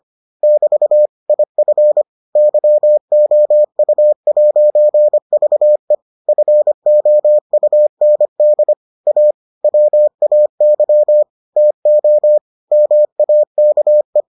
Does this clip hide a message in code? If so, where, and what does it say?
Yes: Morse code from 0.4 to 14.2 s, heard as a keyed tone, about 25 wpm, 600 Hz. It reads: <BT> IF YOU'VE FOUND A WAY TO MAKE